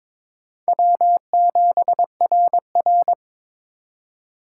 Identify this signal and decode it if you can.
Morse code, heard as a keyed tone, about 22 wpm, 700 Hz. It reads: W7RR
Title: USA Call Signs - 4 Characters - 2x Character Spacing - 22wpm